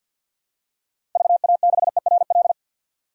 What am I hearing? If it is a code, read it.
Morse code, heard as a keyed tone, about 53 wpm, 700 Hz. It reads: VA6ERL
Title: DXCC-100 Call Signs - 6 Characters - Repeated 2x - 50wpm